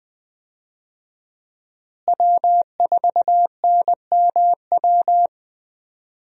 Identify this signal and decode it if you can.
Morse code, heard as a keyed tone, about 20 wpm, 700 Hz. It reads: W4NMW
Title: USA Call Signs - 5 Characters - Repeated 3x - 20wpm